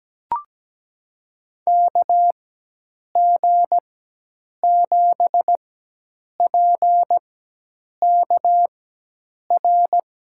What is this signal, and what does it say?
Morse code, heard as a keyed tone, about 17 wpm, spaced as for 8 wpm, 700 Hz. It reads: KG7PKR
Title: USA Call Signs - 6 Characters - 4x Character Spacing - 17wpm